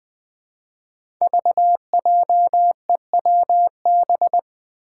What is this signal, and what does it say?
Morse code, heard as a keyed tone, about 21 wpm, 700 Hz. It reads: VJEWB
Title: Five Challenging Letter-Number Combinations 20wpm